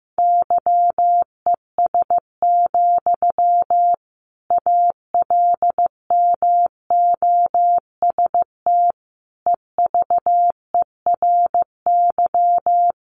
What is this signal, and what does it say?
Morse code, heard as a keyed tone, about 15 wpm, 700 Hz. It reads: YES, ALMOST EVERY
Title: Sentences from Top 700 Words 15wpm